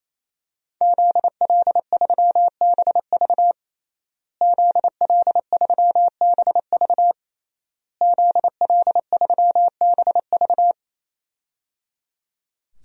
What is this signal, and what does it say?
Morse code, heard as a keyed tone, about 28 wpm, 700 Hz. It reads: ZL3BV ZL3BV ZL3BV
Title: DXCC-100 Call Signs - 5 Characters - Repeated 3x - 28wpm